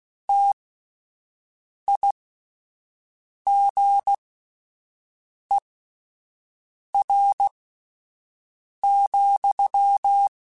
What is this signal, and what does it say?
Morse code, heard as a keyed tone, about 16 wpm, spaced as for 5 wpm, 800 Hz. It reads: TIGER,
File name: Misc / Misc-mp3s / Word Pairs @5